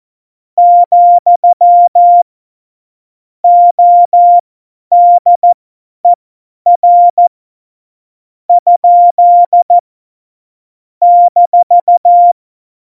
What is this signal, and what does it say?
Morse code, heard as a keyed tone, about 14 wpm, spaced as for 10 wpm, 700 Hz. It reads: , ODER ? -
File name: Morsekurs / Witze / Witz 18. Dezember, 50 F70